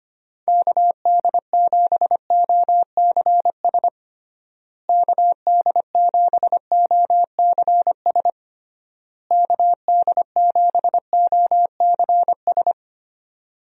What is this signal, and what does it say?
Morse code, heard as a keyed tone, about 25 wpm, 700 Hz. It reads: KD7OCS KD7OCS KD7OCS
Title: USA Call Signs - 6 Characters - Repeated 3x - 25wpm